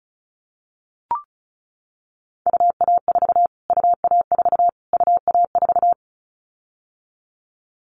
Morse code, heard as a keyed tone, about 34 wpm, 700 Hz. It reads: UA4 UA4 UA4
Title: Callsign Prefixes for Top 25 DXCC - Repeated 3x - 35wpm